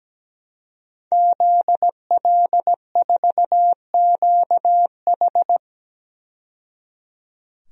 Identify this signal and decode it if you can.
Morse code, heard as a keyed tone, about 17 wpm, 700 Hz. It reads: ZL4QH
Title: DXCC-100 Call Signs - 5 Characters - 17wpm